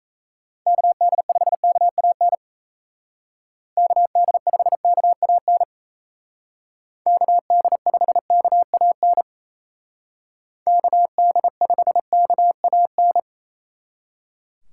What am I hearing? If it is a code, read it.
Morse code, heard as a keyed tone, about 36 wpm, 700 Hz. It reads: KD5KAN KD5KAN KD5KAN KD5KAN